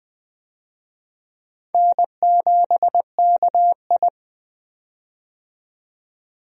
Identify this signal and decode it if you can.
Morse code, heard as a keyed tone, about 20 wpm, 700 Hz. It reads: N7KI
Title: USA Call Signs - 4 Characters - 20wpm